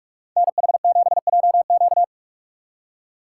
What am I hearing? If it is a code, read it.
Morse code, heard as a keyed tone, about 46 wpm, 700 Hz. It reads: NH7JQ
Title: USA Call Signs - 5 Characters - 4x Character Spacing - 45wpm